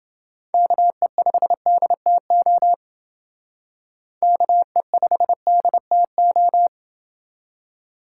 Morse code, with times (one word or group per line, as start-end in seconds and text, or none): 0.5-2.7 KE5DTO
4.2-6.7 KE5DTO